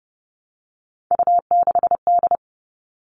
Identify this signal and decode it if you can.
Morse code, heard as a keyed tone, about 30 wpm, 700 Hz. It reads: U6D
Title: All Cut Numbers - 3-Digit 30wpm